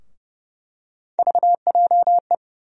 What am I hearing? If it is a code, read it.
Morse code, heard as a keyed tone, about 30 wpm, 700 Hz. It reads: VJE
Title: USA Call Sign Postfixes - 3 Letters 30wpm